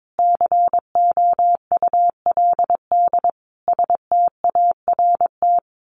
Morse code, heard as a keyed tone, about 22 wpm, 700 Hz. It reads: COULD START
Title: A Sound Of Thunder 22wpm